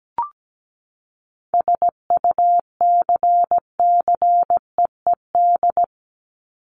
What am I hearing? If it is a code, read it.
Morse code, heard as a keyed tone, about 17 wpm, 700 Hz. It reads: SUCCEED
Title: Top 1501-1750 Words 17wpm